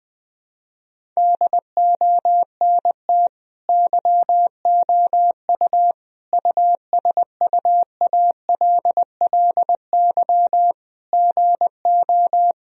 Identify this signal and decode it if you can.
Morse code, heard as a keyed tone, about 20 wpm, 700 Hz. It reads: DONT YOU USUALLY GO
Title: Sentences from Top 1000 Words - Review Part 3 - 20wpm